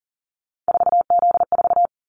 Morse code, heard as a keyed tone, about 40 wpm, 700 Hz. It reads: 4Z4
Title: Callsign Prefixes for Top 50 DXCC - 40wpm